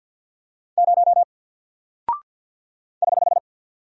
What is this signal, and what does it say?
Morse code, heard as a keyed tone, about 48 wpm, 700 Hz. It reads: 0 <HH>
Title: Single Character (Letter:Number:Punctuation:Prosign) Alt 50wpm